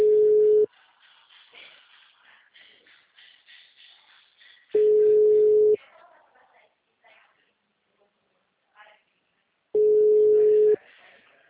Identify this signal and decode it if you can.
Ringback tone (Europe, 425 Hz), with speech.
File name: out- -2004- - - .24